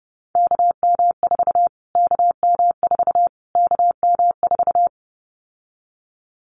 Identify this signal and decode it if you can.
Morse code, heard as a keyed tone, about 30 wpm, 700 Hz. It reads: KM4 KM4 KM4
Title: Callsign Prefixes for Top 25 DXCC - Repeated 3x - 30wpm